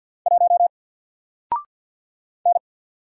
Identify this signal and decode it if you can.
Morse code, heard as a keyed tone, about 48 wpm, 700 Hz. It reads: 1 N
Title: Single Letter-Number Alt 50wpm